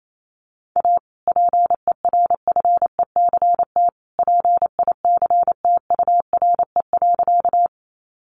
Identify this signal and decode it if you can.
Morse code, heard as a keyed tone, about 28 wpm, 700 Hz. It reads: A PERFECT PICTURE.
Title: Sentences from Top 1000 Words 28wpm